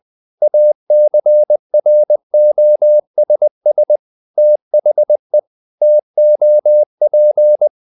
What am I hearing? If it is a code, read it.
Morse code, heard as a keyed tone, about 20 wpm, 600 Hz. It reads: ACROSS THE TOP
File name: the old man and the sea part 2 20wpm